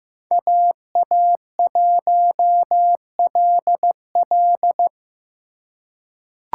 Morse code, heard as a keyed tone, about 15 wpm, 700 Hz. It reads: AA1LL
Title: USA Call Signs - 5 Characters - 2x Character Spacing - 15wpm